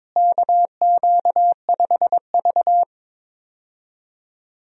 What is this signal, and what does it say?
Morse code, heard as a keyed tone, about 22 wpm, 700 Hz. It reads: KQ5V
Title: USA Call Signs - 4 Characters - Repeated 2x - 22wpm